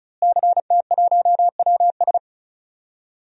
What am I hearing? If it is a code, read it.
Morse code, heard as a keyed tone, about 36 wpm, 700 Hz. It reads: CT1WS
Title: DXCC-100 Call Signs - 5 Characters - Repeated 2x - 35wpm